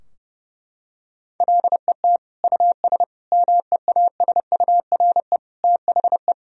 Morse code, heard as a keyed tone, about 32 wpm, 700 Hz. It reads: LET US MEASURE THE
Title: Sentences from Top 1000 Words - Review Part 2 - 30wpm